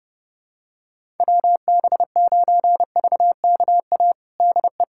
Morse code, heard as a keyed tone, about 30 wpm, 700 Hz. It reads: WB9VKA DE
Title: USA Callsign DE Callsign 30wpm